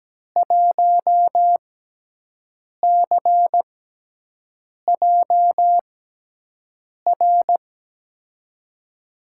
Morse code, heard as a keyed tone, about 17 wpm, spaced as for 6 wpm, 700 Hz. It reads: 1CJR
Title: USA Call Signs - 5 Characters - 6x Character Spacing - 17wpm